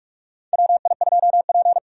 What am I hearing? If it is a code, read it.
Morse code, heard as a keyed tone, about 41 wpm, 700 Hz. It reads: WI2P
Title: USA Call Signs - 4 Characters - 45wpm